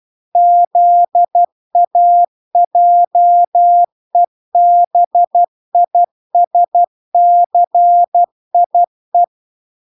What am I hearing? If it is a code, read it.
Morse code, heard as a keyed tone, about 12 wpm, 700 Hz. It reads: ZAJEBISCIE